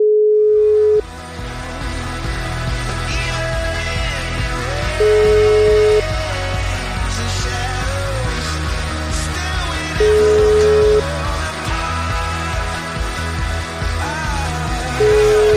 Ringback tone (Europe, 425 Hz), with noise.